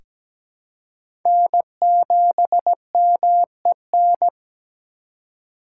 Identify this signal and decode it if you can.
Morse code, heard as a keyed tone, about 17 wpm, 700 Hz. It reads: N7MEN
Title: USA Call Signs - 5 Characters - 3x Character Spacing - 17wpm